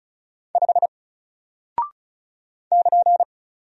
Morse code, heard as a keyed tone, about 36 wpm, 700 Hz. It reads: 5 <KN>
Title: Single Character (Letter:Number:Punctuation:Prosign) 35wpm